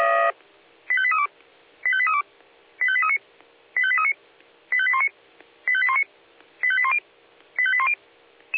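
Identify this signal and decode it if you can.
Five-tone selective call (ZVEI), heard as quick runs of five tones, tones of about 80 ms. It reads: 85832 85832 85839 85839 85819 85819 85810 85810
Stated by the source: ZVEI five-tone call received by SDR.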